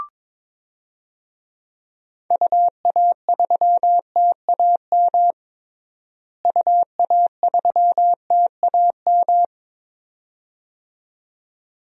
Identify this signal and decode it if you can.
Morse code, heard as a keyed tone, about 22 wpm, 700 Hz. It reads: UA3TAM UA3TAM
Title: DXCC-100 Call Signs - 6 Characters - Repeated 2x - 22wpm